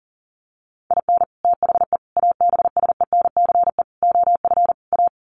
Morse code, heard as a keyed tone, about 40 wpm, 700 Hz. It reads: IN THE ABSENCE OF A
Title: Sets of 5 Words Encore 40wpm